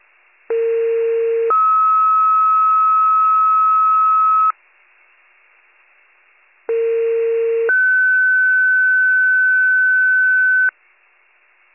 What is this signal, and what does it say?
Two-tone page, heard as a one-second tone, then a three-second tone, about 460 then 1290 Hz, about 460 then 1500 Hz. It is heard from 0.5 to 10.7 s.